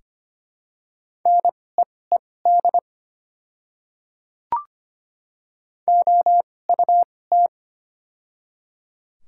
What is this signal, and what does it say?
Morse code, heard as a keyed tone, about 25 wpm, spaced as for 18 wpm, 700 Hz. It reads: NEED OUT
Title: 2x Character Spacing - Top 200 Words - 25wpm